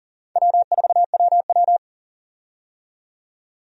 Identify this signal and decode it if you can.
Morse code, heard as a keyed tone, about 39 wpm, 690 Hz. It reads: W4WW
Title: USA Call Signs - 4 Characters - 40wpm